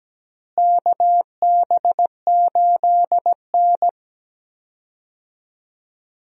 Morse code, heard as a keyed tone, about 17 wpm, 700 Hz. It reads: KB8N
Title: USA Call Signs - 4 Characters - Repeated 3x - 17wpm